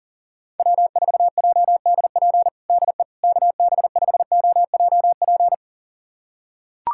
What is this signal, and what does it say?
Morse code, heard as a keyed tone, about 40 wpm, 700 Hz. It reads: W4JDP DE KB5OJP
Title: USA Callsign DE Callsign 40wpm